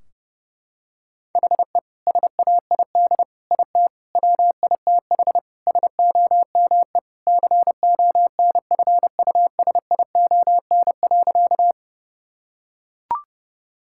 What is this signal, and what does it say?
Morse code, heard as a keyed tone, about 32 wpm, 700 Hz. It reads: HE SAID IT WITH SOME CONFUSION.
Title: Sentences from Top 1600 Words 30wpm